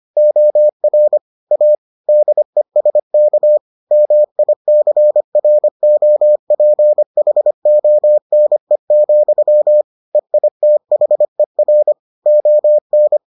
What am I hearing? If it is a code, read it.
Morse code, heard as a keyed tone, about 25 wpm, 600 Hz. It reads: OR A DESK MICROPHONE, EITHER ON